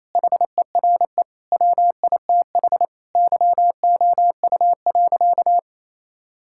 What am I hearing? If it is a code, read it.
Morse code, heard as a keyed tone, about 28 wpm, 700 Hz. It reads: HERE WITH YOU.
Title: Sentences from Top 1600 Words 28wpm